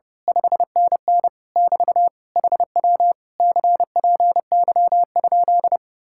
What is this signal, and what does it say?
Morse code, heard as a keyed tone, about 28 wpm, 700 Hz. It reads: 5NN <BT> HW CPY?
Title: Speed-Racing Common QSO Phrases Encore - 30wpm